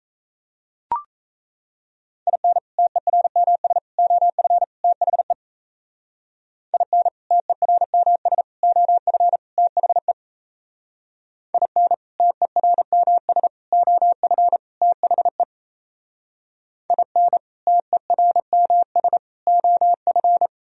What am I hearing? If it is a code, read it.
Morse code, heard as a keyed tone, about 34 wpm, 700 Hz. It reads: IN TERMS OF THE IN TERMS OF THE IN TERMS OF THE IN TERMS OF
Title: Speed-Racing - Sets of 4 Words 28wpm